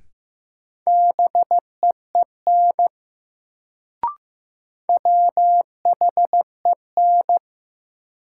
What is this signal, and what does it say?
Morse code, heard as a keyed tone, about 15 wpm, 700 Hz. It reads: BEEN WHEN
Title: Sentences from Top 100 Words - New Word Drill - 15wpm